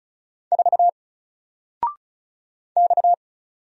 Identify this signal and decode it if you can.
Morse code, heard as a keyed tone, about 34 wpm, 700 Hz. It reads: 4 X
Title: Single Letter-Number 35wpm